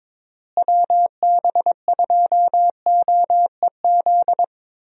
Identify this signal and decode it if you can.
Morse code, heard as a keyed tone, about 22 wpm, 700 Hz. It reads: WB2OEZ
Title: USA Call Signs - 6 Characters - Repeated 2x - 22wpm